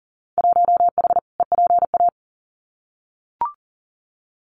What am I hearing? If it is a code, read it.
Morse code, heard as a keyed tone, about 40 wpm, 700 Hz. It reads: 1H EPA
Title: Winter Field Day Exchanges 40wpm